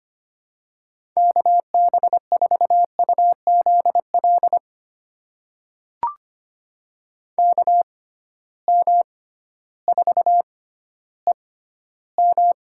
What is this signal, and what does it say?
Morse code, heard as a keyed tone, about 25 wpm, spaced as for 13 wpm, 700 Hz. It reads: KB4UZL KM4EM
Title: USA Call Signs - 6x Character Spacing - 25wpm